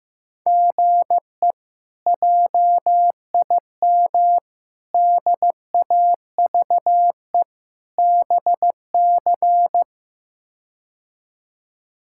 Morse code, heard as a keyed tone, about 15 wpm, 700 Hz. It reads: GE JIM DAVE BC
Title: K1USN SST QSO Practice - 15wpm